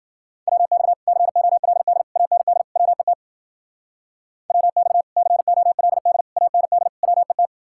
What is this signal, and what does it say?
Morse code, heard as a keyed tone, about 48 wpm, 700 Hz. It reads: WX COLD AND WET WX COLD AND WET
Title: Speed-Racing Common QSO Phrases - 40wpm